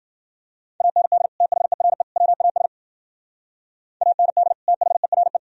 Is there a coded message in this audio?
Yes: Morse code from 0.8 to 5.4 s, heard as a keyed tone, about 53 wpm, 700 Hz. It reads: AND THERE WAS AND THERE